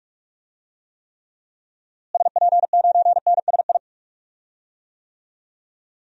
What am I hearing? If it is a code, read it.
Morse code, heard as a keyed tone, about 46 wpm, 700 Hz. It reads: SP9NSI
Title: DXCC-100 Call Signs - 6 Characters - Repeated 2x - 45wpm